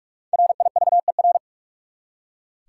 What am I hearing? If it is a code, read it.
Morse code, heard as a keyed tone, about 41 wpm, 700 Hz. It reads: RIVER
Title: Five Letter Words - 45wpm